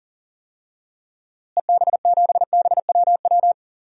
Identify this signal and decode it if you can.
Morse code, heard as a keyed tone, about 40 wpm, 700 Hz. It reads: EB7BWW
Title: DXCC-100 Call Signs - 6 Characters - Repeated 3x - 40wpm